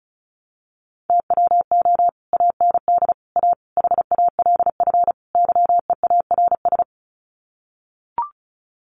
Morse code, heard as a keyed tone, about 34 wpm, 700 Hz. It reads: TWO AND A HALF YEARS
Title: Sets of 5 Words 35wpm